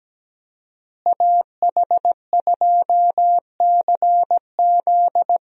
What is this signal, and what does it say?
Morse code, heard as a keyed tone, about 17 wpm, 700 Hz. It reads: AH2CZ
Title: USA Call Signs - 5 Characters - Repeated 3x - 17wpm